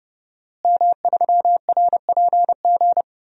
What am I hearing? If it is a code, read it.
Morse code, heard as a keyed tone, about 30 wpm, 700 Hz. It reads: M3RPG
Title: DXCC-100 Call Signs - 5 Characters - Repeated 2x - 30wpm